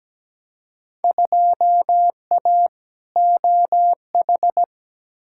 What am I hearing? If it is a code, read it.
Morse code, heard as a keyed tone, about 17 wpm, 700 Hz. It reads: 2A OH
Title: Field Day Exchanges - 17wpm